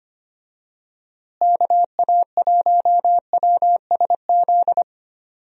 Morse code, heard as a keyed tone, about 25 wpm, 700 Hz. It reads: KA1WSZ
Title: USA Call Signs - 6 Characters - Repeated 2x - 25wpm